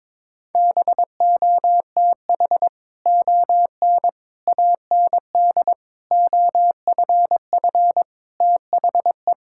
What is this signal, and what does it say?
Morse code, heard as a keyed tone, about 22 wpm, 700 Hz. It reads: BOTH ON AND OFF THE
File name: practice / 22wpm / Sets of 6 Words - 22wpm